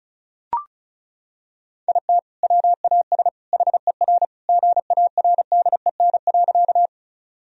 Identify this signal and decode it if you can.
Morse code, heard as a keyed tone, about 36 wpm, 700 Hz. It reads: IT WAS HER GARDEN.
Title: Sentences from Top 1000 Words - Review Part 4 - 35wpm